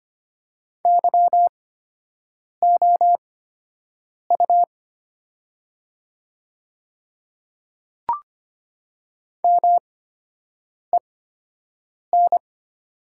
Morse code, heard as a keyed tone, about 25 wpm, spaced as for 7 wpm, 700 Hz. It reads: YOU MEN
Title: 8x Character Spacing - Top 200 Words - 25wpm